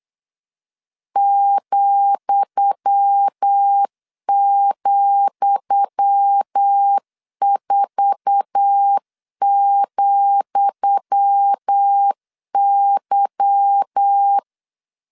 Morse code, heard as a keyed tone, about 8 wpm, 800 Hz. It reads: ,,4,Y
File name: LEZIONI / 10 SIGNS / 10D COMA 8wpm 3749